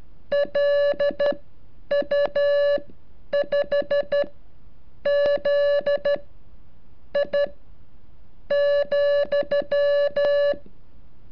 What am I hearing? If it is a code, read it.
Morse code, heard as a keyed tone, about 13 wpm, spaced as for 7 wpm, 570 Hz. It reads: LU5ZI,
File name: curso / ejercicio 36